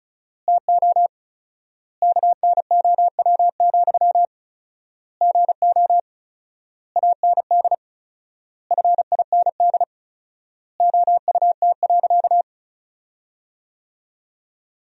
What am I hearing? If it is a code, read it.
Morse code, heard as a keyed tone, about 34 wpm, 700 Hz. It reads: TO KNOW, GO AND FIND OUT.
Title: 4x Word-Spacing - Sentences from Top 200 Words - 35wpm